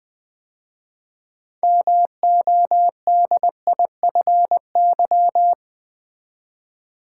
Morse code, heard as a keyed tone, about 20 wpm, 700 Hz. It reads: MODIFY
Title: 6 Letter Words - First 3 Letters Given - 20wpm